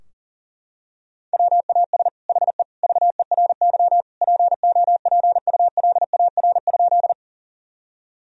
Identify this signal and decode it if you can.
Morse code, heard as a keyed tone, about 40 wpm, 700 Hz. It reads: WAS HE VERY POPULAR?
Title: Sentences from Top 1000 Words - Review Part 3 - 40wpm